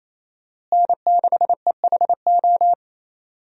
Morse code, heard as a keyed tone, about 27 wpm, 700 Hz. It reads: N6EHO
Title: USA Call Signs - 5 Characters - Repeated 2x - 28wpm